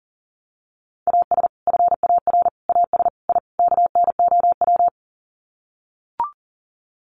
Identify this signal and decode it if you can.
Morse code, heard as a keyed tone, about 37 wpm, 700 Hz. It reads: AS FAR AS I KNOW
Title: Sets of 5 Words 40wpm